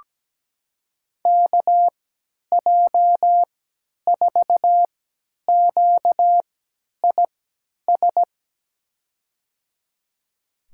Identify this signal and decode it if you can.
Morse code, heard as a keyed tone, about 17 wpm, spaced as for 10 wpm, 700 Hz. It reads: KJ4QIS
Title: USA Call Signs - 6 Characters - 3x Character Spacing - 17wpm